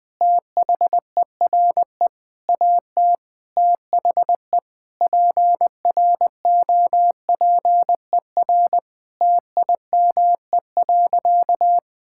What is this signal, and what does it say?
Morse code, heard as a keyed tone, about 21 wpm, 700 Hz. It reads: THERE AT THE PROPER TIME.
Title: Sentences from Top 1000 Words - Review Part 5 - 20wpm